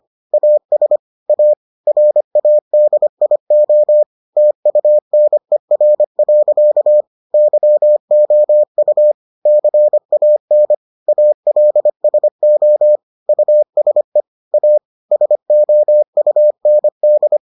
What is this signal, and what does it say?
Morse code, heard as a keyed tone, about 25 wpm, 600 Hz. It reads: AS A RADIO TUNER. YOU CAN ALSO USE A SOUND